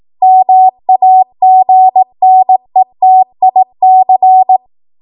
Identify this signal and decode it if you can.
Morse code, heard as a keyed tone, about 18 wpm, 750 Hz. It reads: MAGNETIC
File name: Archive / 18 WPM / 18WPM